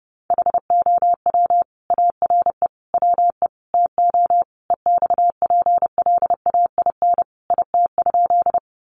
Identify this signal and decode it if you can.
Morse code, heard as a keyed tone, about 30 wpm, 700 Hz. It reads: HOW ARE WE TO EXPLAIN IT?
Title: Sentences from Top 1100 Words 30wpm